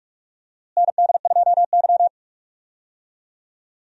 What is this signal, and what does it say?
Morse code, heard as a keyed tone, about 46 wpm, 700 Hz. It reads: ND2Y
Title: USA Call Signs - 4 Characters - Repeated 2x - 45wpm